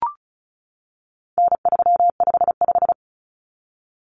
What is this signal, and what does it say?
Morse code, heard as a keyed tone, about 34 wpm, 700 Hz. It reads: N355
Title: Common Cut Numbers - 4-Digit 35wpm